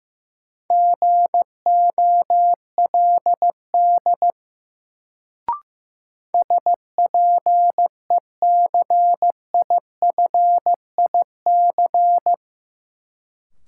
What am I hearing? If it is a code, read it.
Morse code, heard as a keyed tone, about 15 wpm, 700 Hz. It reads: GOLD SPECIFIC
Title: Sentences from Top 1000 Words - Review Part 3 - New Word Drill - 15wpm